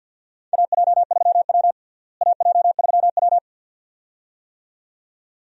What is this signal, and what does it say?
Morse code, heard as a keyed tone, about 52 wpm, 700 Hz. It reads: AJ3W AJ3W
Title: USA Call Signs - 4 Characters - Repeated 2x - 50wpm